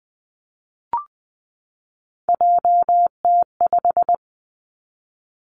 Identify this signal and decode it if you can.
Morse code, heard as a keyed tone, about 19 wpm, 700 Hz. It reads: JT5
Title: Three Challenging Letter-Number Combinations 20wpm